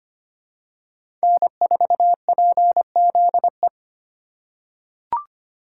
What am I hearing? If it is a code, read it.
Morse code, heard as a keyed tone, about 25 wpm, 700 Hz. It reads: N4PZE
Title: USA Call Signs - 5 Characters - 8x Character Spacing - 25wpm